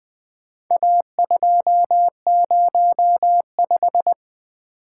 Morse code, heard as a keyed tone, about 20 wpm, 700 Hz. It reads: A205
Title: Four Letter-Number Combinations - 20wpm